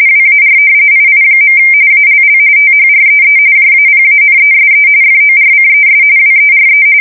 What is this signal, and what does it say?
RTTY, about 45.45 baud. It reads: WAR II. ALL THREE JAPANESE FORCES WERE
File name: fsk170 / battleshipnj